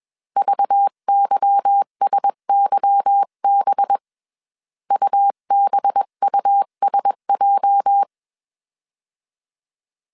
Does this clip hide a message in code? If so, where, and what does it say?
Yes: Morse code from 0.4 to 8.0 s, heard as a keyed tone, about 21 wpm, 800 Hz. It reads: VYSYB UBUSJ